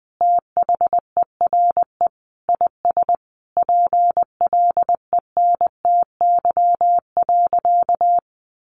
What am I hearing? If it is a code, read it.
Morse code, heard as a keyed tone, about 19 wpm, 700 Hz. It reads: THERE IS PLENTY.
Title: Sentences from Top 1300 Words 20wpm